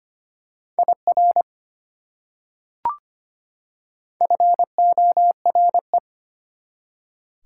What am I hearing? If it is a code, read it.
Morse code, heard as a keyed tone, about 25 wpm, 700 Hz. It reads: IR FORE
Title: Prefixes and Suffixes 25wpm